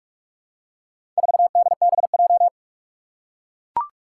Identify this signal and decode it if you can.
Morse code, heard as a keyed tone, about 41 wpm, 700 Hz. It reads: 4DBJ
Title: Four Challenging Letter-Number Combinations 45wpm